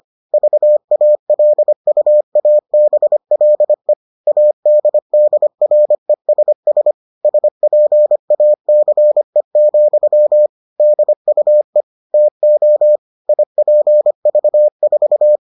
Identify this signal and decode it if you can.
Morse code, heard as a keyed tone, about 25 wpm, 600 Hz. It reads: VALUABLE ADDRESS SPACE, DUE TO IPV4